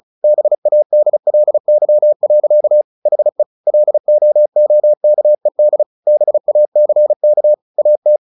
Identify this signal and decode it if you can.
Morse code, heard as a keyed tone, about 34 wpm, 600 Hz. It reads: BADLY. HE LOOKED BACK AT
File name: the old man and the sea 2 35wpm